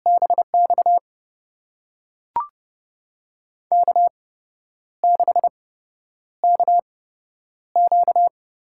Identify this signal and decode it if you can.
Morse code, heard as a keyed tone, about 30 wpm, spaced as for 10 wpm, 700 Hz. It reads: BX K6KQ
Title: USA Call Signs - 8x Character Spacing - 30wpm